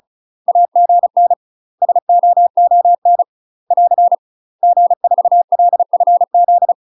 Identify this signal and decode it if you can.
Morse code, heard as a keyed tone, about 34 wpm, 700 Hz. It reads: AGN SOON <AR> G4LFZ
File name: UK MASTER QSOs TRAINING / 35WPM / UK MASTER QSOs 0040